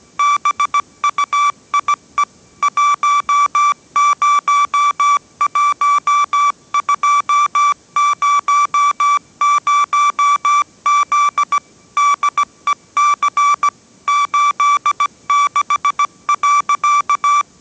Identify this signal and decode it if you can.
Morse code, heard as a keyed tone, about 17 wpm, 1160 Hz. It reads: BUIE 101200Z DEC 86.
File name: 18wpm